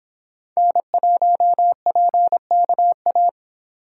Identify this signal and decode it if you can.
Morse code, heard as a keyed tone, about 26 wpm, 700 Hz. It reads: N1PKA